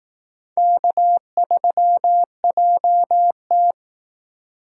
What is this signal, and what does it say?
Morse code, heard as a keyed tone, about 18 wpm, 700 Hz. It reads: K3JT
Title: Speed Racing - USA 1x2 Call Signs 15wpm